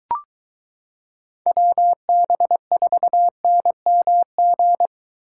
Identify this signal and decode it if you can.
Morse code, heard as a keyed tone, about 23 wpm, 700 Hz. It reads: WB4NMG